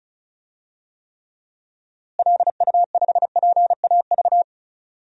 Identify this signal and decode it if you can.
Morse code, heard as a keyed tone, about 36 wpm, 700 Hz. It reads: LU5PAV
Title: DXCC-100 Call Signs - 6 Characters - Repeated 2x - 35wpm